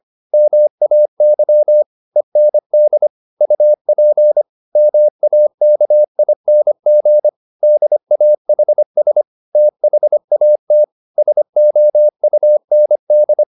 Morse code, heard as a keyed tone, about 25 wpm, 600 Hz. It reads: MAY END UP MAKING DAHS THAT SOUND